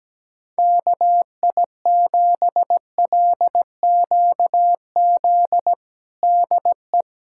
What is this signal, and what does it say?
Morse code, heard as a keyed tone, about 17 wpm, 700 Hz. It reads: KI7LQZ DE
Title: USA Callsign DE Callsign 17wpm